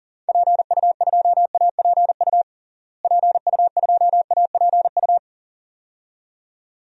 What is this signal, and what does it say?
Morse code, heard as a keyed tone, about 40 wpm, 700 Hz. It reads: PU2APU PU2APU
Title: DXCC-100 Call Signs - 6 Characters - Repeated 2x - 40wpm